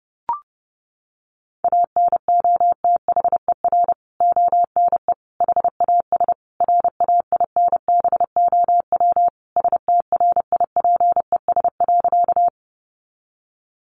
Morse code, heard as a keyed tone, about 30 wpm, 700 Hz. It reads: ANOTHER ONE HAS RAINBOW STRIPES.
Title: Additional Easy Phrases 30wpm